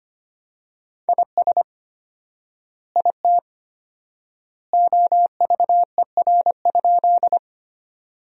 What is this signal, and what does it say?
Morse code, heard as a keyed tone, about 25 wpm, 700 Hz. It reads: IS IT OVER?
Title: 4x Word-Spacing - Sentences from Top 100 Words - 25wpm